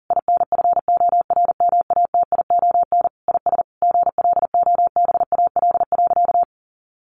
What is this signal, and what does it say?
Morse code, heard as a keyed tone, about 40 wpm, 700 Hz. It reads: INFORMATION IS GLOBAL.
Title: Sentences from Top 1200 Words 40wpm